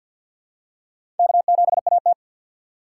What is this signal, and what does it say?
Morse code, heard as a keyed tone, about 53 wpm, 700 Hz. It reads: K7AT
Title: USA Call Signs - 4 Characters - 3x Character Spacing - 50wpm